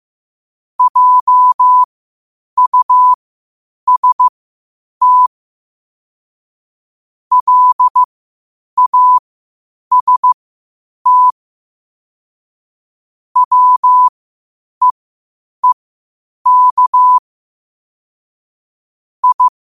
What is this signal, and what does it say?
Morse code, heard as a keyed tone, about 15 wpm, spaced as for 8 wpm, 1000 Hz. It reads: JUST LAST WEEK I
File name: qotdfiles / 8wpm